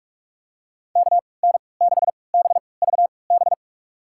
Morse code, heard as a keyed tone, about 42 wpm, spaced as for 25 wpm, 690 Hz. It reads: KN6BVB
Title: USA Call Signs - 6 Characters - 3x Character Spacing - 45wpm